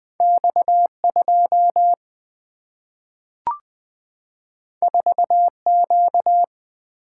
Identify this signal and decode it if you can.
Morse code, heard as a keyed tone, about 20 wpm, 700 Hz. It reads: X2 4Q
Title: Two Letter-Number Combinations - 20wpm